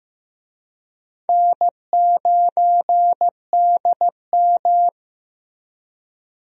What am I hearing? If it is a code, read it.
Morse code, heard as a keyed tone, about 15 wpm, 700 Hz. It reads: N9DM
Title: USA Call Signs - 4 Characters - Repeated 2x - 15wpm